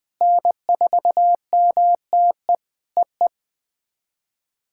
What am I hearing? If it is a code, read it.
Morse code, heard as a keyed tone, about 20 wpm, 700 Hz. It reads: N4MTE EE
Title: POTA QSO Practice - 20wpm